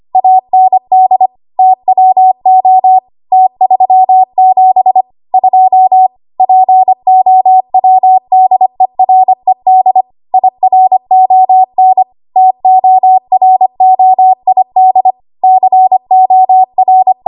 Morse code, heard as a keyed tone, about 25 wpm, 750 Hz. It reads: AND TWO T37 2 POWDERED IRON TOROID COR
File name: Archive / 25 WPM / 25WPM